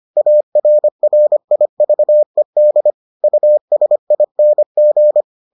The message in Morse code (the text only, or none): ARRIVED USING